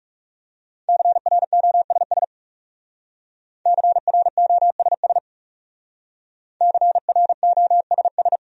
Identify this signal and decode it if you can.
Morse code, heard as a keyed tone, about 36 wpm, 700 Hz. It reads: CROSS CROSS CROSS
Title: Speed-Racing Top 500 Words 30wpm